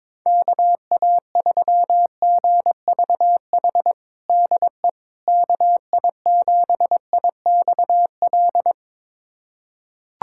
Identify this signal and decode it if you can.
Morse code, heard as a keyed tone, about 22 wpm, 700 Hz. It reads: KA3GVH DE KI7IXL
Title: USA Callsign DE Callsign - 6 Characters - 22wpm